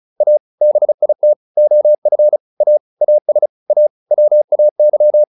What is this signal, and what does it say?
Morse code, heard as a keyed tone, about 34 wpm, 600 Hz. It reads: A BIT OF A AS A WAY